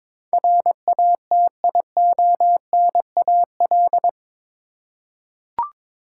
Morse code, heard as a keyed tone, about 22 wpm, 700 Hz. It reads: RATIONAL
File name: practice / 22wpm / Eight Letter Words - 22wpm